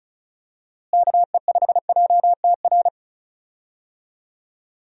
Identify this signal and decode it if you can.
Morse code, heard as a keyed tone, about 34 wpm, 700 Hz. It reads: KE5JTR
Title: USA Call Signs - 6 Characters - Repeated 3x - 35wpm